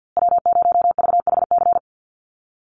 Morse code, heard as a keyed tone, about 49 wpm, 700 Hz. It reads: W0VHC
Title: USA Call Signs Repeated 3x 50wpm